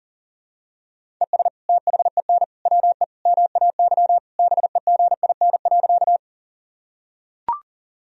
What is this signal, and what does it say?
Morse code, heard as a keyed tone, about 40 wpm, 700 Hz. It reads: ES THEN WE MAY BEGIN.
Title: Sentences from Top 300 Words and Abbreviations 40wpm